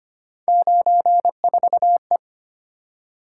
Morse code, heard as a keyed tone, about 30 wpm, 700 Hz. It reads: 94E
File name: Speed-Racing - Three Letter-Number Combinations - 25wpm